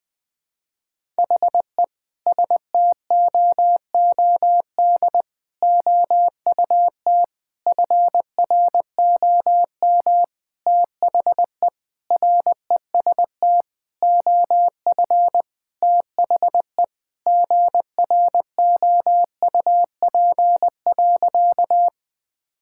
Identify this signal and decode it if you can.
Morse code, heard as a keyed tone, about 20 wpm, 700 Hz. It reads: HE STOOD OUT FROM THE REST OF THE GROUP.
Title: Sentences with 5-Word Phrases - Spoken Beginning - 20wpm